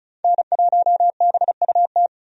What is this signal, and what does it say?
Morse code, heard as a keyed tone, about 36 wpm, 700 Hz. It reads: N1BUT
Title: USA Call Signs - 5 Characters - 35wpm